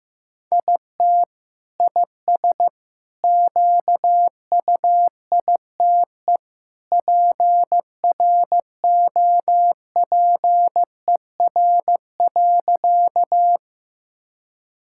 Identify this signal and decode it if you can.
Morse code, heard as a keyed tone, about 15 wpm, 700 Hz. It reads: IT IS QUITE PROPER.
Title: Sentences from Top 1000 Words - Review Part 5 - 15wpm